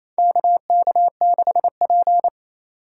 Morse code, heard as a keyed tone, about 28 wpm, 700 Hz. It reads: KK6P
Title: USA Call Signs - 4 Characters - 3x Character Spacing - 28wpm